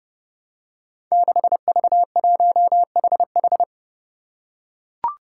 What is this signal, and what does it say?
Morse code, heard as a keyed tone, about 30 wpm, 700 Hz. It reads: 6V1HH
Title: Five Challenging Letter-Number Combinations 30wpm